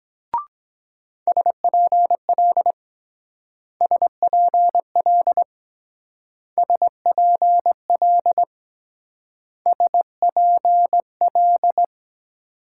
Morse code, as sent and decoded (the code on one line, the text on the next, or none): ... .--. .-.. / ... .--. .-.. / ... .--. .-.. / ... .--. .-..
SPL SPL SPL SPL